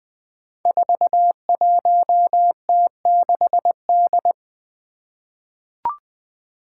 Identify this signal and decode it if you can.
Morse code, heard as a keyed tone, about 20 wpm, 700 Hz. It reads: 41T6D
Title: Five Challenging Letter-Number Combinations 20wpm